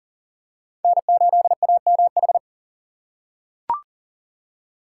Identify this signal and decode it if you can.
Morse code, heard as a keyed tone, about 40 wpm, 700 Hz. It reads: N8AMH
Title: USA Call Signs - 5 Characters - 6x Character Spacing - 40wpm